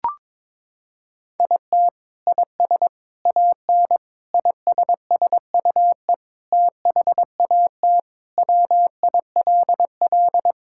Morse code, heard as a keyed tone, about 22 wpm, 700 Hz. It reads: IT IS AN ISSUE THAT WILL
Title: Sentences from Top 1000 Words - Review Part 6 - 22wpm